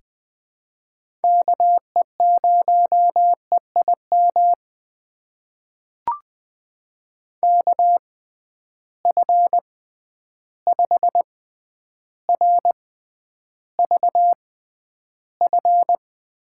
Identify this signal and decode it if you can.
Morse code, heard as a keyed tone, about 20 wpm, spaced as for 10 wpm, 700 Hz. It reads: KE0EIM KF5RVF
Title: USA Call Signs - 6 Characters - 6x Character Spacing - 20wpm